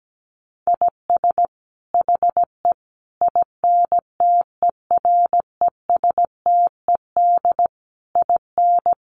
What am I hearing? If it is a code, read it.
Morse code, heard as a keyed tone, about 17 wpm, 700 Hz. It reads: IS HE INTERESTED IN
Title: Sentences from Top 900 Words 17wpm